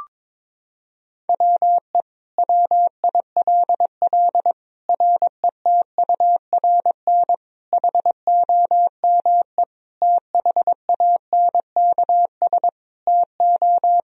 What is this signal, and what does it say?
Morse code, heard as a keyed tone, about 22 wpm, 700 Hz. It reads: WE WILL RETURN HOME THANKS TO
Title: Sentences from Top 1000 Words - Review Part 3 - 22wpm